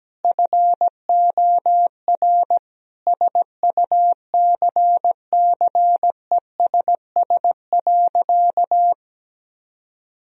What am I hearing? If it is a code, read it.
Morse code, heard as a keyed tone, about 17 wpm, 700 Hz. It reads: FOR SUCCESS.
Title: Sentences from Top 1000 Words - Review Part 3 - 17wpm